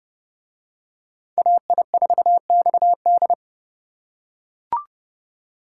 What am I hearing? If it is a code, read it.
Morse code, heard as a keyed tone, about 29 wpm, 700 Hz. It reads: AI4XD